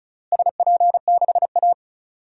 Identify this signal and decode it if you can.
Morse code, heard as a keyed tone, about 34 wpm, 700 Hz. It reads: SP6A
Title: DXCC-100 Call Signs - 4 Characters - Repeated 2x - 35wpm